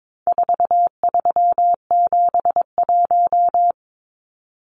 Morse code, heard as a keyed tone, about 22 wpm, 700 Hz. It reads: 4371
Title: Numbers - 4-Digit 22wpm